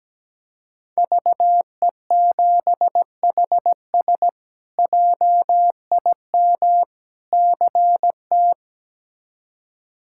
Morse code, heard as a keyed tone, about 17 wpm, 700 Hz. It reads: VE7HS JIM CT
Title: K1USN SST QSO Practice - 17wpm